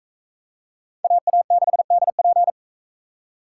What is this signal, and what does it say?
Morse code, heard as a keyed tone, about 44 wpm, 700 Hz. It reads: AA6DP